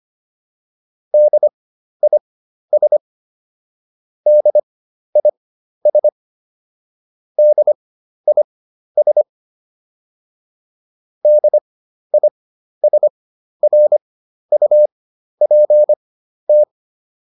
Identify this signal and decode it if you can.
Morse code, heard as a keyed tone, about 25 wpm, spaced as for 12 wpm, 600 Hz. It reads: DIS DIS DIS DISRUPT